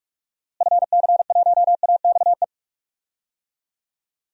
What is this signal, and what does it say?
Morse code, heard as a keyed tone, about 44 wpm, 700 Hz. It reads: FC1AXE
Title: DXCC-100 Call Signs - 6 Characters - Repeated 3x - 45wpm